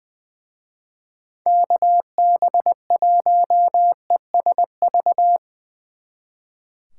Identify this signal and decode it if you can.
Morse code, heard as a keyed tone, about 23 wpm, 700 Hz. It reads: KB1ESV
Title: Speed Racing - USA 2x3 Call Signs 20wpm